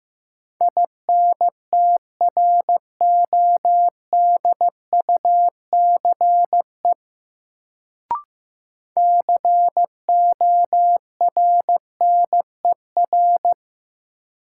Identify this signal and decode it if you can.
Morse code, heard as a keyed tone, about 15 wpm, 700 Hz. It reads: INTRODUCE CORNER
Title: Sentences from Top 1600 Words - New Word Drill Alt - 15wpm